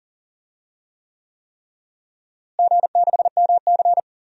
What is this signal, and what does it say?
Morse code, heard as a keyed tone, about 40 wpm, 700 Hz. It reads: G6MC
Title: DXCC-100 Call Signs - 4 Characters - Repeated 3x - 40wpm